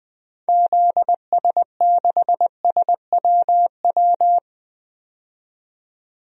Morse code, heard as a keyed tone, about 20 wpm, 700 Hz. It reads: ZS6SWW
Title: DXCC-100 Call Signs - 6 Characters - 20wpm